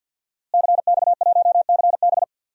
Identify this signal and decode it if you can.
Morse code, heard as a keyed tone, about 48 wpm, 700 Hz. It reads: CX1CB
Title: DXCC-100 Call Signs - 5 Characters - Repeated 3x - 50wpm